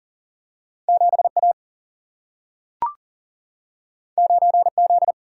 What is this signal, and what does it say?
Morse code, heard as a keyed tone, about 39 wpm, 700 Hz. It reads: 7A 9Z
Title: Single-Letter Single-Number 40wpm